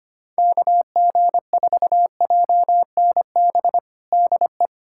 Morse code, heard as a keyed tone, about 25 wpm, 700 Hz. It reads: KG4JNB DE
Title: USA Callsign DE Callsign 25wpm